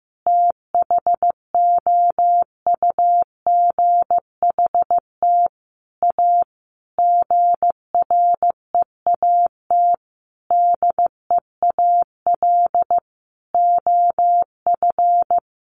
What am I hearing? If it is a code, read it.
Morse code, heard as a keyed tone, about 15 wpm, 700 Hz. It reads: THOUGHT A GREAT DEAL OF
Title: Sentences from Top 800 Words 15wpm